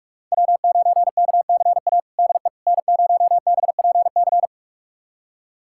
Morse code, heard as a keyed tone, about 42 wpm, 700 Hz. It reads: W9KCA DE N0BPC
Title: USA Callsign DE Callsign - 5 Characters - 45wpm